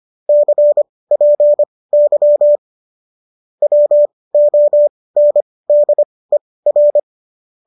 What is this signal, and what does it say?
Morse code, heard as a keyed tone, about 25 wpm, spaced as for 18 wpm, 590 Hz. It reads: CPY WONDER
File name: qso218 18